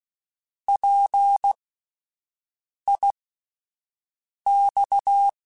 Morse code, heard as a keyed tone, about 16 wpm, spaced as for 5 wpm, 800 Hz. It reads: PIX